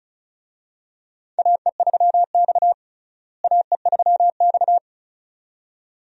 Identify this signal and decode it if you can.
Morse code, heard as a keyed tone, about 36 wpm, 700 Hz. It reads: AE3X AE3X
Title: USA Call Signs - 4 Characters - Repeated 2x - 35wpm